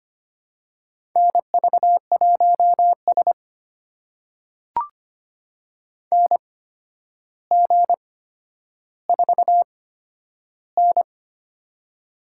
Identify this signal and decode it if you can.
Morse code, heard as a keyed tone, about 25 wpm, spaced as for 11 wpm, 700 Hz. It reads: NV1S NG4N
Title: USA Call Signs - 4 Characters - 8x Character Spacing - 25wpm